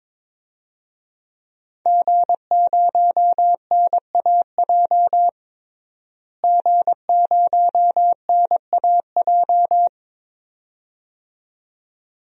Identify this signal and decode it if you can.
Morse code, heard as a keyed tone, about 22 wpm, 700 Hz. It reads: G0NAJ G0NAJ
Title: DXCC-100 Call Signs - 5 Characters - Repeated 2x - 22wpm